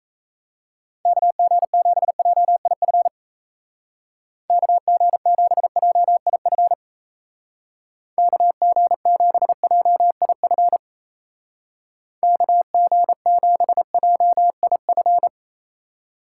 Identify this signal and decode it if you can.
Morse code, heard as a keyed tone, about 35 wpm, 700 Hz. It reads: KG7JIF KG7JIF KG7JIF KG7JIF